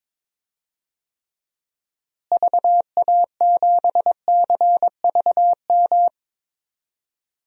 Morse code, heard as a keyed tone, about 22 wpm, 700 Hz. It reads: VA7CVM
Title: DXCC-100 Call Signs - 6 Characters - Repeated 2x - 22wpm